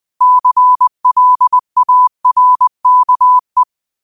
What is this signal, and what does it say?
Morse code, heard as a keyed tone, about 20 wpm, 1000 Hz. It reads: CLARKE
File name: qotdfiles / 20wpm